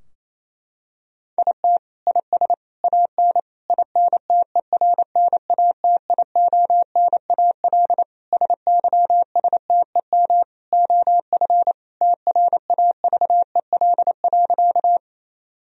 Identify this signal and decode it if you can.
Morse code, heard as a keyed tone, about 28 wpm, 700 Hz. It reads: IT IS AN INTERNATIONAL SYSTEM OF TRAVEL.
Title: Sentences from Top 1000 Words - Review Part 4 - 28wpm